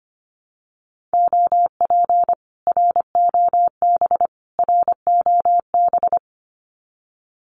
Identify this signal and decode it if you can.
Morse code, heard as a keyed tone, about 25 wpm, 700 Hz. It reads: OP ROB ROB
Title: Common QSO Phrases Encore 25wpm